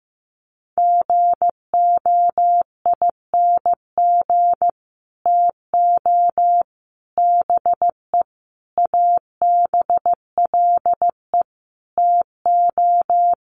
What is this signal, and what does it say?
Morse code, heard as a keyed tone, about 15 wpm, 700 Hz. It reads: GOING TO BE ABLE TO
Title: Sets of 5 Words 15wpm